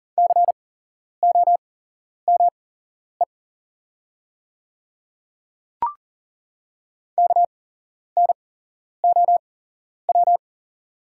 Morse code, heard as a keyed tone, about 39 wpm, spaced as for 11 wpm, 700 Hz. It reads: COME KNOW
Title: 8x Character Spacing - Top 100 Words - 40wpm